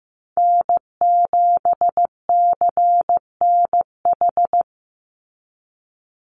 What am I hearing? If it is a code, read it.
Morse code, heard as a keyed tone, about 15 wpm, 700 Hz. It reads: N7CNH
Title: NA SOTA Chaser Callsigns 15wpm